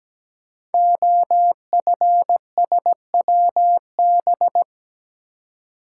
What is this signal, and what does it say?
Morse code, heard as a keyed tone, about 17 wpm, 700 Hz. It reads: OFSWB
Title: Code Groups - 5 Letters - 17wpm